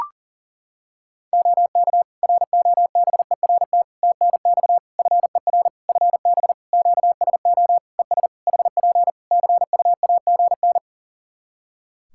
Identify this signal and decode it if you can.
Morse code, heard as a keyed tone, about 40 wpm, 700 Hz. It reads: OK ROBERT TNX FER FB QSO ES HP CUAGN
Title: K1USN SST QSO Practice 40wpm